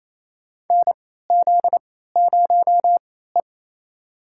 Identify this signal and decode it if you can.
Morse code, heard as a keyed tone, about 28 wpm, spaced as for 16 wpm, 700 Hz. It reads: NZ0E
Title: USA Call Signs - 4 Characters - 3x Character Spacing - 28wpm